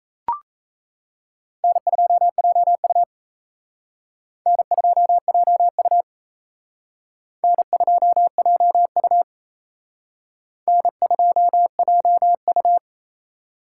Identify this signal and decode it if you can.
Morse code, heard as a keyed tone, about 36 wpm, 700 Hz. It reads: N2JU N2JU N2JU N2JU